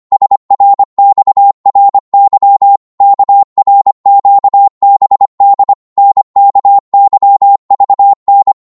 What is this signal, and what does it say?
Morse code, heard as a keyed tone, about 25 wpm, 800 Hz. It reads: SRXRY KRQBD NKYVN